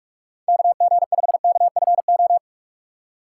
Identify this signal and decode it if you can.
Morse code, heard as a keyed tone, about 46 wpm, 700 Hz. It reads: KG5KFO
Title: USA Call Signs - 6 Characters - 2x Character Spacing - 45wpm